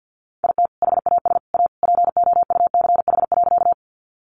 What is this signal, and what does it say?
Morse code, heard as a keyed tone, about 52 wpm, 700 Hz. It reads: IT HAS A POUCH.
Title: Easy Phrases 50wpm